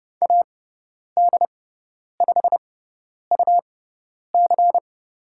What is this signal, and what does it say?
Morse code, heard as a keyed tone, about 30 wpm, spaced as for 10 wpm, 700 Hz. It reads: AD5UC
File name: Lesson 17g - Letter u - Call Signs 30wpm